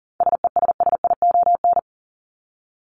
Morse code, heard as a keyed tone, about 39 wpm, 700 Hz. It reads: SESSION
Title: Top 1501-1750 Words 40wpm